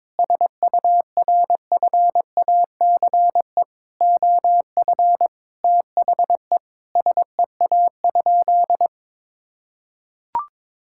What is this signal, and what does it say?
Morse code, heard as a keyed tone, about 22 wpm, 700 Hz. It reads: SURFACE OF THE SEA?
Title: Sentences from Top 1000 Words - Review Part 5 - 22wpm